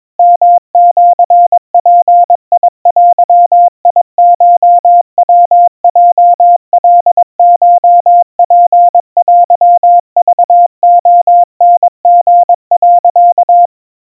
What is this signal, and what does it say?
Morse code (Japanese Wabun code), heard as a keyed tone, about 22 wpm, 700 Hz. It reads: ヨシヅデコヤヲカコツテクレタリ、
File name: 80 cpm 700 Hz Crispy